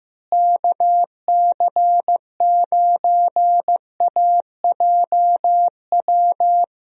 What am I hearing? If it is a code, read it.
Morse code, heard as a keyed tone, about 15 wpm, 700 Hz. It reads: KC9AJW
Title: USA Call Signs - 6 Characters - Repeated 3x - 15wpm